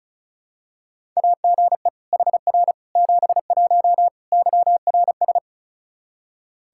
Morse code, heard as a keyed tone, about 34 wpm, 700 Hz. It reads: AGE HR 71 YRS
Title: Practice QSO - Exchange 3 - More Info - 35wpm